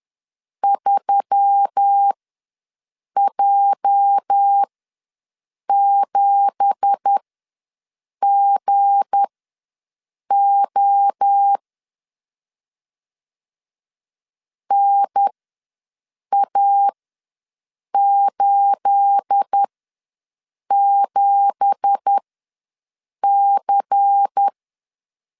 Morse code, heard as a keyed tone, about 10 wpm, spaced as for 6 wpm, 800 Hz. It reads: 3J7GO NA87C
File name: LEZIONI / 9 2867 ws 10wpm 4215